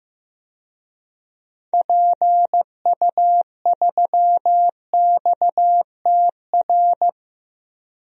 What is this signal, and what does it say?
Morse code, heard as a keyed tone, about 15 wpm, 700 Hz. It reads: PU3XTR
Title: DXCC-100 Call Signs - 6 Characters - Repeated 2x - 15wpm